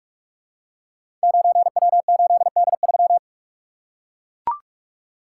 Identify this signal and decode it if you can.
Morse code, heard as a keyed tone, about 44 wpm, 690 Hz. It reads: 9W8D3
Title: Code Groups - 5 Letter:Numbers - 45wpm